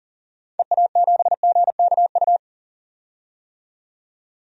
Morse code, heard as a keyed tone, about 40 wpm, 700 Hz. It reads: EA7GKU
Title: DXCC-100 Call Signs - 6 Characters - 40wpm